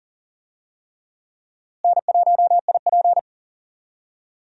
Morse code, heard as a keyed tone, about 40 wpm, 700 Hz. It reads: N1IP
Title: USA Call Signs - 4 Characters - Repeated 3x - 40wpm